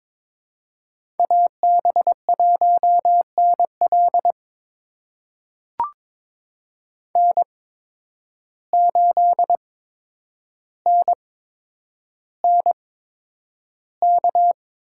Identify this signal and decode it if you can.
Morse code, heard as a keyed tone, about 22 wpm, spaced as for 9 wpm, 700 Hz. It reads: AB1NL N8NNK
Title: USA Call Signs - 5 Characters - 8x Character Spacing - 22wpm